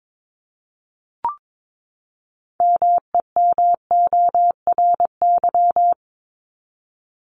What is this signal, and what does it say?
Morse code, heard as a keyed tone, about 22 wpm, 700 Hz. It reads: MEMORY
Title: Top 1251-1500 Words 22wpm